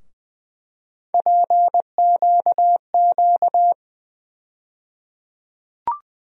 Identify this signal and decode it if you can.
Morse code, heard as a keyed tone, about 20 wpm, 700 Hz. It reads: PQQ
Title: USA Call Sign Postfixes - 3 Letters 20wpm